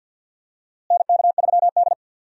Morse code, heard as a keyed tone, about 47 wpm, 690 Hz. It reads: NK3D
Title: USA Call Signs - 4 Characters - Repeated 2x - 50wpm